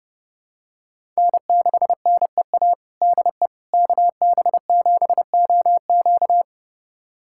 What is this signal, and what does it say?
Morse code, heard as a keyed tone, about 30 wpm, 700 Hz. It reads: N6NEA DE KB7OQ
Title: USA Callsign DE Callsign - 5 Characters - 30wpm